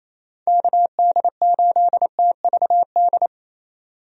Morse code, heard as a keyed tone, about 28 wpm, 700 Hz. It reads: KD8TVD
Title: USA Call Signs - 6 Characters - 28wpm